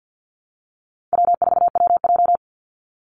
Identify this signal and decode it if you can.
Morse code, heard as a keyed tone, about 50 wpm, 700 Hz. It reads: W4WJ
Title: Easy Call Signs - 50wpm